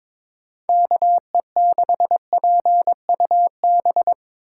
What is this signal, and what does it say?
Morse code, heard as a keyed tone, about 24 wpm, 700 Hz. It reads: KE6PUB
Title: Speed Racing - USA Call Signs 22wpm